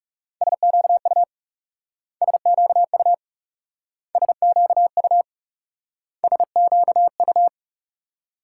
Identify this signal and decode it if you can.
Morse code, heard as a keyed tone, about 37 wpm, 700 Hz. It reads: SQU SQU SQU SQU
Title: Speed-Racing Multi-letter Phonemes - 30wpm